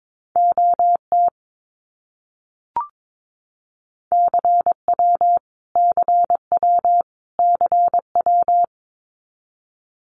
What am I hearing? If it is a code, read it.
Morse code, heard as a keyed tone, about 22 wpm, 700 Hz. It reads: OT CW CW CW
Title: Abbreviations Repeated 3x - 22wpm